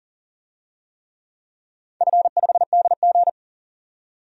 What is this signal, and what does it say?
Morse code, heard as a keyed tone, about 37 wpm, 700 Hz. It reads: F5DG
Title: DXCC-100 Call Signs - 4 Characters - 40wpm